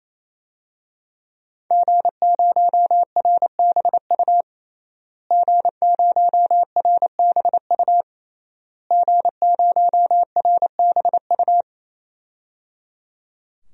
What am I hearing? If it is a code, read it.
Morse code, heard as a keyed tone, about 28 wpm, 700 Hz. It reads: G0RBU G0RBU G0RBU
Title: DXCC-100 Call Signs - 5 Characters - Repeated 3x - 28wpm